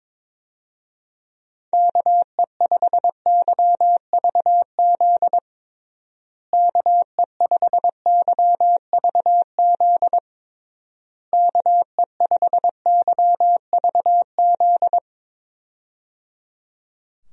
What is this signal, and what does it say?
Morse code, heard as a keyed tone, about 22 wpm, 700 Hz. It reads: KE5YVZ KE5YVZ KE5YVZ
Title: USA Call Signs - 6 Characters - Repeated 3x - 22wpm